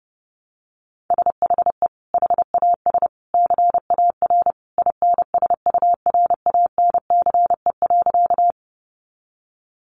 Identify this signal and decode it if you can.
Morse code, heard as a keyed tone, about 30 wpm, 700 Hz. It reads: SHE HAS CAR INSURANCE.
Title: Sentences from Top 1100 Words 30wpm